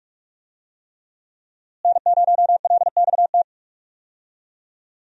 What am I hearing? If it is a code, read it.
Morse code, heard as a keyed tone, about 46 wpm, 700 Hz. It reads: N0LXT
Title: USA Call Signs - 5 Characters - Repeated 3x - 45wpm